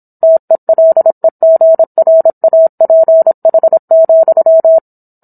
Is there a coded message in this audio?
Yes: Morse code from 0.2 to 4.8 s, heard as a keyed tone, about 27 wpm, 650 Hz. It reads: TELEGRAPH,